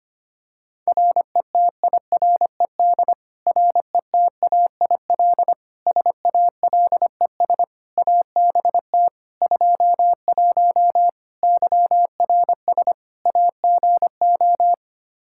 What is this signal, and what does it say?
Morse code, heard as a keyed tone, about 25 wpm, 700 Hz. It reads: RETIRED RETAIL SALES ABT 21 YRS AGO
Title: Practice QSO - Exchange 3 - More Info - 25wpm